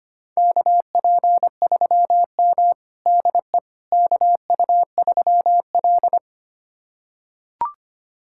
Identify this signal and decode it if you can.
Morse code, heard as a keyed tone, about 25 wpm, 700 Hz. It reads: KP3M DE KU3L
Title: USA Callsign DE Callsign - 4 Characters - 25wpm